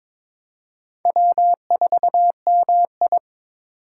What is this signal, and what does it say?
Morse code, heard as a keyed tone, about 22 wpm, 700 Hz. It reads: W4MI
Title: USA Call Signs - 2x Character Spacing - 22wpm